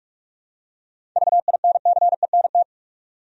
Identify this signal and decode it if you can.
Morse code, heard as a keyed tone, about 46 wpm, 700 Hz. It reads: VINCENT
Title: USA Common Names - 45wpm